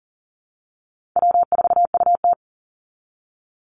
Morse code, heard as a keyed tone, about 39 wpm, 700 Hz. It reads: W4UT
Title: Easy Call Signs Repeated 2x - 40wpm